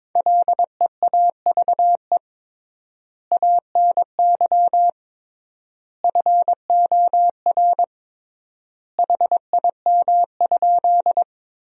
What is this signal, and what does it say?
Morse code, heard as a keyed tone, about 22 wpm, 700 Hz. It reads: LEAVE ANY FOR HIM?
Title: 3x Word-Spacing - Sentences from Top 400 Words - 22wpm